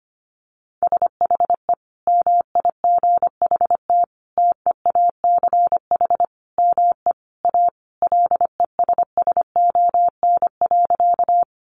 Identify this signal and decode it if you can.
Morse code, heard as a keyed tone, about 25 wpm, 700 Hz. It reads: SHE MIGHT TEACH ME A LESSON.
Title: Sentences from Top 1200 Words 25wpm